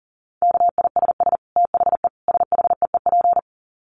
Morse code, heard as a keyed tone, about 40 wpm, 700 Hz. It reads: KISS THE SHEEP
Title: Dit Challenge 40wpm